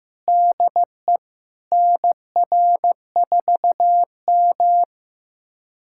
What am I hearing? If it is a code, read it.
Morse code, heard as a keyed tone, about 16 wpm, 700 Hz. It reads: DE NR4M
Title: USA Callsign DE Callsign - 4 Characters - 15wpm